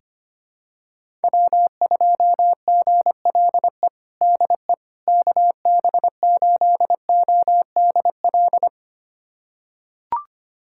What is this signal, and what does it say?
Morse code, heard as a keyed tone, about 25 wpm, 700 Hz. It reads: W2GLE DE KB8ODL
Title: USA Callsign DE Callsign 25wpm